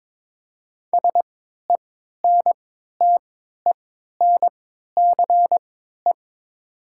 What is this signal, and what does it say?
Morse code, heard as a keyed tone, about 22 wpm, spaced as for 12 wpm, 700 Hz. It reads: SENTENCE
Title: 3x Character Spacing - Top 300 Words - 22wpm